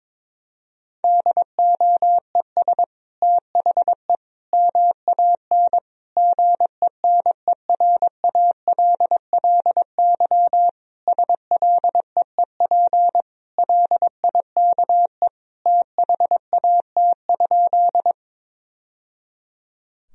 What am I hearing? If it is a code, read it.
Morse code, heard as a keyed tone, about 22 wpm, 700 Hz. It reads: DOES THE MAN GENERALLY SLEEP LIKE THAT?
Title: Sentences from Top 1000 Words - Review Part 4 - 22wpm